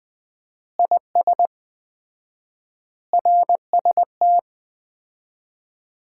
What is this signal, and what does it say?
Morse code, heard as a keyed tone, about 21 wpm, 700 Hz. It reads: IS RST
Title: Common QSO Phrases - 4x Word-Spacing - 20wpm